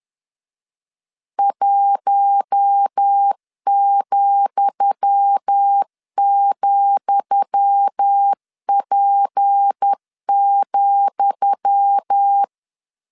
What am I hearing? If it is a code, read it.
Morse code, heard as a keyed tone, about 10 wpm, 800 Hz. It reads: 1,,P,
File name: LEZIONI / 10 SIGNS / 10D COMA 10wpm 3013